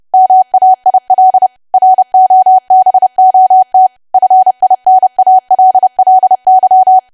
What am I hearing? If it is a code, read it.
Morse code, heard as a keyed tone, about 30 wpm, 750 Hz. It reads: MAIL ROBOT FINALLY
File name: cwa / W1AW-Level3 / 30WPM